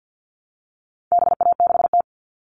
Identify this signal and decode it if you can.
Morse code, heard as a keyed tone, about 50 wpm, 700 Hz. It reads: BA6T
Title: All Cut Numbers - 4-Digit 50wpm